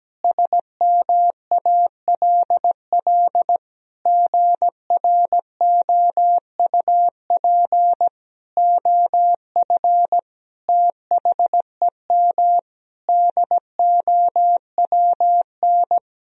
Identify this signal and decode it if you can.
Morse code, heard as a keyed tone, about 17 wpm, 700 Hz. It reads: SMALL GROUP OF THEM DOWN
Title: Sentences from Top 1000 Words - Review Part 1 - 17wpm